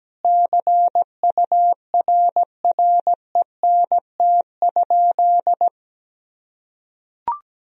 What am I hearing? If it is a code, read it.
Morse code, heard as a keyed tone, about 17 wpm, 700 Hz. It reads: CURRENT?
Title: Sentences from Top 1000 Words - Review Part 4 - 17wpm